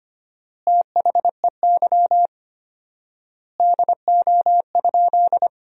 Morse code, heard as a keyed tone, about 25 wpm, 700 Hz. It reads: THEY DO?
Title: 4x Word-Spacing - Sentences from Top 100 Words - 25wpm